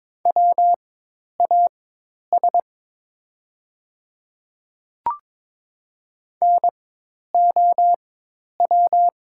Morse code, heard as a keyed tone, about 22 wpm, spaced as for 10 wpm, 700 Hz. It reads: WAS NOW
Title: 4x Character Spacing - Top 100 Words - 22wpm